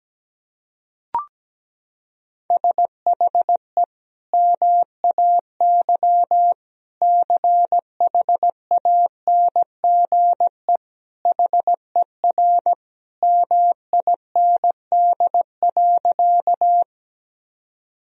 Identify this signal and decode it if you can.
Morse code, heard as a keyed tone, about 17 wpm, 700 Hz. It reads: SHE MAY CHANGE HER MIND.
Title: Sentences from Top 1000 Words - Review Part 1 - 17wpm